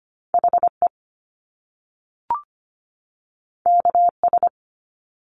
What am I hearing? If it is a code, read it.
Morse code, heard as a keyed tone, about 25 wpm, 700 Hz. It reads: HE KS
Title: Top Two Letter Combinations in Words 25wpm